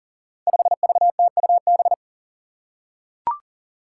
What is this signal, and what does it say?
Morse code, heard as a keyed tone, about 40 wpm, 700 Hz. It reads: 5VTUB
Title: Five Challenging Letter-Number Combinations 40wpm